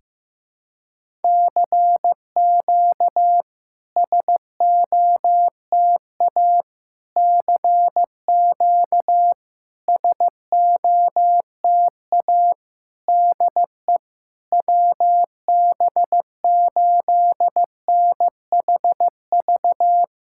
Morse code, heard as a keyed tone, about 15 wpm, 700 Hz. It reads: CQ SOTA CQ SOTA DE WB8NHV
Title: SOTA QSO Practice 15wpm